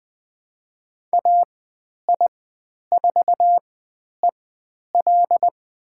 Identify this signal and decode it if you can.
Morse code, heard as a keyed tone, about 20 wpm, spaced as for 10 wpm, 700 Hz. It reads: AI4EL
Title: 109 Lesson 18g - Letter m - Call Signs 20wpm